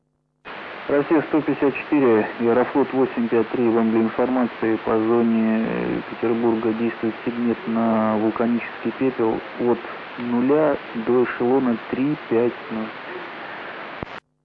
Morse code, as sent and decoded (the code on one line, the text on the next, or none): none